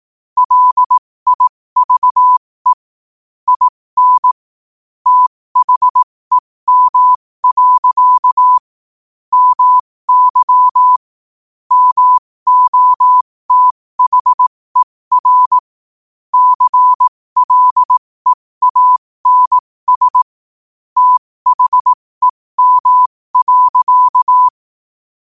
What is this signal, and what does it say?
Morse code, heard as a keyed tone, about 18 wpm, spaced as for 15 wpm, 1000 Hz. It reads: LIVE IN THEM. MY MOTHER CLEANS THEM.
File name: qotdfiles / 15wpm